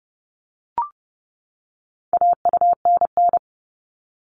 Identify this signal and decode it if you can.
Morse code, heard as a keyed tone, about 30 wpm, 700 Hz. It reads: AUNN
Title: All Cut Numbers - 4-Digit 30wpm